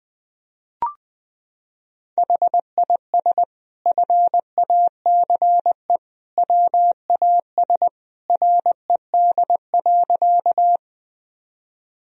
Morse code, heard as a keyed tone, about 21 wpm, 700 Hz. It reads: HIS FACE WAS RED.
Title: Sentences from Top 1000 Words - Review Part 2 - 20wpm